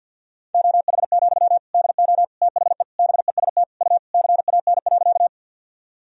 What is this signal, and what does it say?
Morse code, heard as a keyed tone, about 46 wpm, 700 Hz. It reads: OH, DO THE BEST U CAN.
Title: Sentences from Top 500 Words and Abbreviations 50wpm